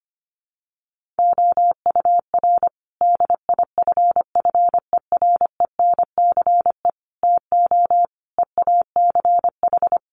Morse code, heard as a keyed tone, about 25 wpm, 700 Hz. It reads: OUR DIFFERENCE TO EACH
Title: Sentences from Top 1500 Words 25wpm